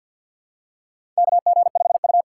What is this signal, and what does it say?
Morse code, heard as a keyed tone, about 52 wpm, 700 Hz. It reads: KG5U
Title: USA Call Signs - 3x Character Spacing - 50wpm